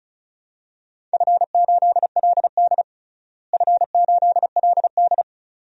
Morse code, heard as a keyed tone, about 36 wpm, 690 Hz. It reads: F8LD F8LD
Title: DXCC-100 Call Signs - 4 Characters - Repeated 2x - 35wpm